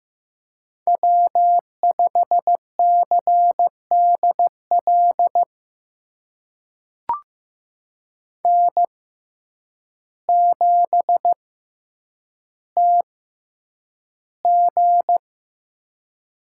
Morse code, heard as a keyed tone, about 15 wpm, spaced as for 8 wpm, 700 Hz. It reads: W5CDL N7TG
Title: USA Call Signs - 5 Characters - 6x Character Spacing - 15wpm